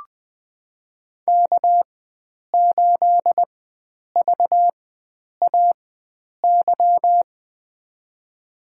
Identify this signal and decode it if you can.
Morse code, heard as a keyed tone, about 21 wpm, spaced as for 9 wpm, 700 Hz. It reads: K8VAY
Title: USA Call Signs - 5 Characters - 4x Character Spacing - 20wpm